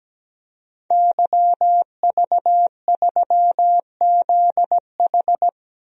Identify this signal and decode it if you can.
Morse code, heard as a keyed tone, about 17 wpm, 700 Hz. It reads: YV3ZH
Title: DXCC-100 Call Signs - 5 Characters - 17wpm